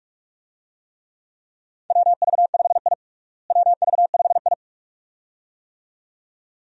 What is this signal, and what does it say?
Morse code, heard as a keyed tone, about 46 wpm, 700 Hz. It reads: WV5I WV5I
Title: USA Call Signs - 4 Characters - Repeated 2x - 45wpm